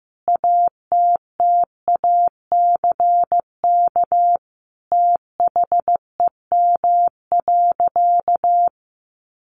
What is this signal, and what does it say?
Morse code, heard as a keyed tone, about 15 wpm, 700 Hz. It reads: ATTACK THEM.
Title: Sentences from Top 1500 Words 15wpm